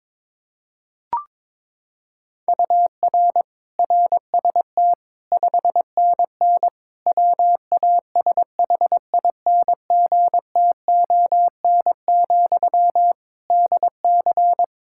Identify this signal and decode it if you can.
Morse code, heard as a keyed tone, about 22 wpm, 700 Hz. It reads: UR RST 5NN WASHINGTON, DC
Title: Practice QSO - Exchange 1 - Beginning - 22wpm